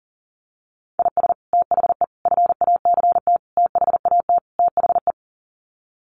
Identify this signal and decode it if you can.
Morse code, heard as a keyed tone, about 40 wpm, 700 Hz. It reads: IS THE FACT THAT THE
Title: Sets of 5 Words Encore2 40wpm